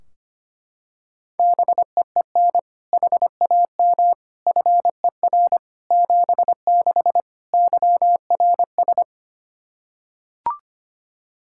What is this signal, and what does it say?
Morse code, heard as a keyed tone, about 25 wpm, 700 Hz. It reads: BEEN HAM FER 76 YRS
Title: Practice QSO - Exchange 3 - More Info - 25wpm